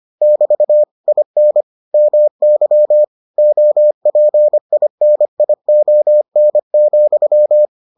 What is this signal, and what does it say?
Morse code, heard as a keyed tone, about 25 wpm, 600 Hz. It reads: <BT> IN MY OPINION,